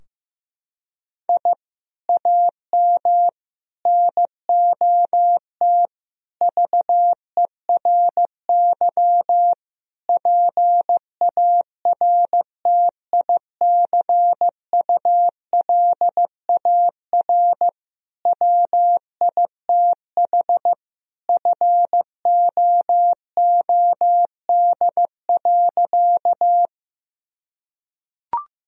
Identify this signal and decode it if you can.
Morse code, heard as a keyed tone, about 15 wpm, 700 Hz. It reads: I AM NOT VERY PARTICULAR WITH FOOD.
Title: Sentences from Top 1000 Words - Review Part 4 - 15wpm